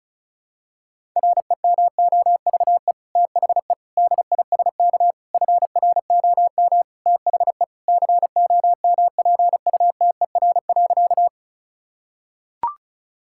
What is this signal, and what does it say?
Morse code, heard as a keyed tone, about 36 wpm, 700 Hz. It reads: REMOVE THE DISK FROM THE COMPUTER.
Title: Sentences from Top 1000 Words - Review Part 6 - 35wpm